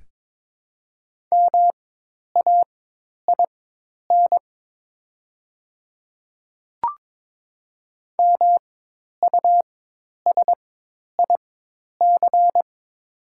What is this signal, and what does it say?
Morse code, heard as a keyed tone, about 22 wpm, spaced as for 10 wpm, 700 Hz. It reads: MAIN MUSIC
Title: 4x Character Spacing - Top 300 Words - 22wpm